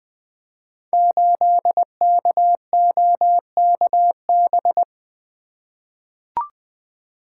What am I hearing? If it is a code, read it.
Morse code, heard as a keyed tone, about 20 wpm, 700 Hz. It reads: 8KOKB
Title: Code Groups - 5 Letter:Numbers - 20wpm